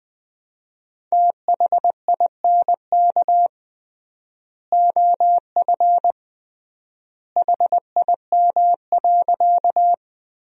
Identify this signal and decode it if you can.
Morse code, heard as a keyed tone, about 20 wpm, 700 Hz. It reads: THINK OF HIM.
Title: 3x Word-Spacing - Sentences from Top 200 Words - 20wpm